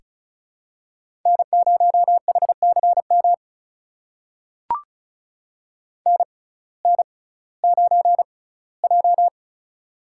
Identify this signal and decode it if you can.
Morse code, heard as a keyed tone, about 34 wpm, spaced as for 19 wpm, 700 Hz. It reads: N0HCM NN9J
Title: USA Call Signs - 6x Character Spacing - 35wpm